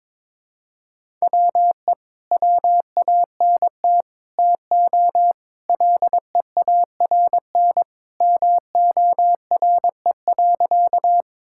Morse code, heard as a keyed tone, about 22 wpm, 700 Hz. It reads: WE WANT TO LEARN MORE.
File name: practice / 22wpm / Sentences with 3-Word Phrases - 22wpm